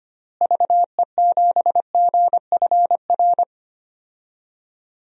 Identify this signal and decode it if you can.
Morse code, heard as a keyed tone, about 25 wpm, 700 Hz. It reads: VE7GFR
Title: DXCC-100 Call Signs - 6 Characters - Repeated 3x - 25wpm